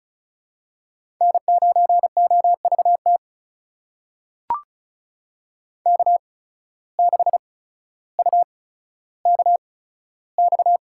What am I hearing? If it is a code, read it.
Morse code, heard as a keyed tone, about 36 wpm, spaced as for 15 wpm, 700 Hz. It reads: N9OVT K6UKX
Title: USA Call Signs - 5 Characters - 8x Character Spacing - 35wpm